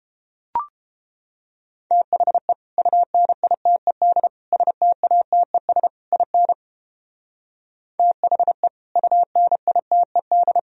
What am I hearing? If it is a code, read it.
Morse code, heard as a keyed tone, about 30 wpm, 700 Hz. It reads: THE UNITED STATES IN THE UNITED